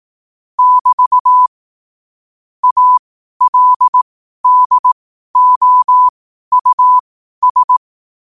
Morse code, heard as a keyed tone, about 18 wpm, spaced as for 12 wpm, 1000 Hz. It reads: <BT> ALDOUS
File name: qotdfiles / 12wpm